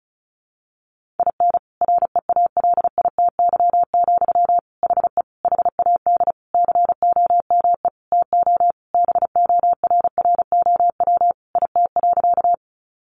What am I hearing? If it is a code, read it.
Morse code, heard as a keyed tone, about 34 wpm, 700 Hz. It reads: IN REALITY, HE HAD COME TO BORROW IT.
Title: Sentences from Top 1500 Words 35wpm